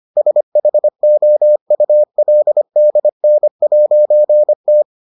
Morse code, heard as a keyed tone, about 25 wpm, 600 Hz. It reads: SHOULDN'T